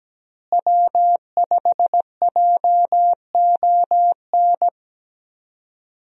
Morse code, heard as a keyed tone, about 17 wpm, 700 Hz. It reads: W5JON
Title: USA Call Signs - 5 Characters - Repeated 2x - 17wpm